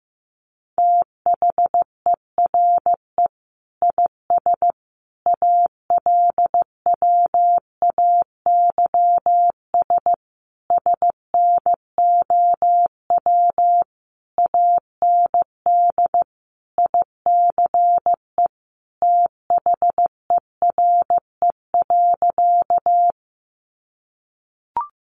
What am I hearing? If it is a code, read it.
Morse code, heard as a keyed tone, about 15 wpm, 700 Hz. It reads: THERE IS ALWAYS SNOW AND ICE THERE.
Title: Sentences from Top 1000 Words 15wpm